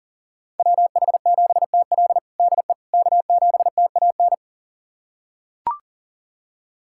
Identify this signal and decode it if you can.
Morse code, heard as a keyed tone, about 40 wpm, 700 Hz. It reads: WH7TL DE K7TAN
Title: USA Callsign DE Callsign - 5 Characters - 40wpm